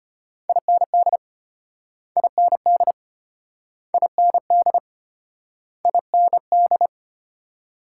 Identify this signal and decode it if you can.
Morse code, heard as a keyed tone, about 34 wpm, 700 Hz. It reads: IND IND IND IND